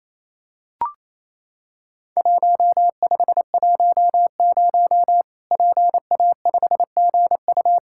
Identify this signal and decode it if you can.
Morse code, heard as a keyed tone, about 27 wpm, 700 Hz. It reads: 1510 PA5GU
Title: Worked All Europe DX Contest - QTC Exchange - 28wpm